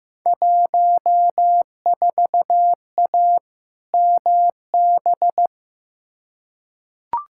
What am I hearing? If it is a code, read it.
Morse code, heard as a keyed tone, about 15 wpm, 700 Hz. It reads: 14A MB
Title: Field Day Exchanges Encore - 15wpm